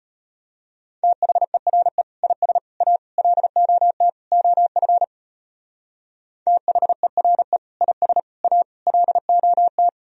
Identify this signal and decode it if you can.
Morse code, heard as a keyed tone, about 33 wpm, 700 Hz. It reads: THERE IS A LOT OF THERE IS A LOT